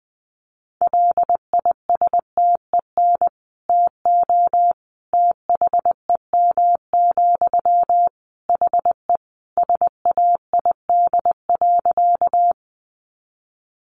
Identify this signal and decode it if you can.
Morse code, heard as a keyed tone, about 20 wpm, 700 Hz. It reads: LISTEN TO THEM, HE SAID.
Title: Sentences from Top 400 Words 20wpm